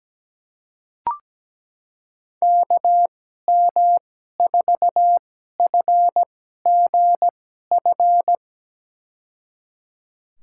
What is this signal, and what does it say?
Morse code, heard as a keyed tone, about 17 wpm, spaced as for 12 wpm, 700 Hz. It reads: KM4FGF
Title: USA Call Signs - 6 Characters - 2x Character Spacing - 17wpm